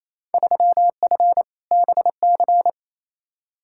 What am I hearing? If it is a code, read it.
Morse code, heard as a keyed tone, about 28 wpm, 700 Hz. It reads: 3F BC
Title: Field Day Exchanges Encore - 28wpm